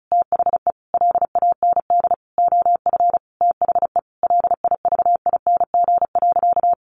Morse code, heard as a keyed tone, about 34 wpm, 700 Hz. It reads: THE LAND OF THE LIVING.
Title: Sentences from Top 1100 Words 35wpm